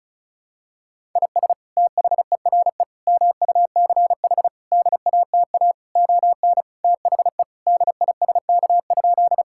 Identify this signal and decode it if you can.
Morse code, heard as a keyed tone, about 34 wpm, 690 Hz. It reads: IS THERE MUCH DATA ON THE DISK?
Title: Sentences from Top 1000 Words - Review Part 6 - 35wpm